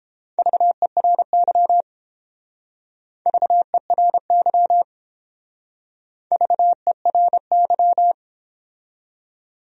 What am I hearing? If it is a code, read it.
Morse code, heard as a keyed tone, about 30 wpm, 700 Hz. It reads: VERY VERY VERY